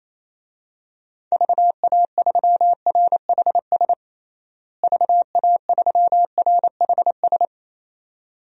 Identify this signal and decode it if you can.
Morse code, heard as a keyed tone, about 28 wpm, 700 Hz. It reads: VA3RHS VA3RHS
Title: DXCC-100 Call Signs - 6 Characters - Repeated 2x - 28wpm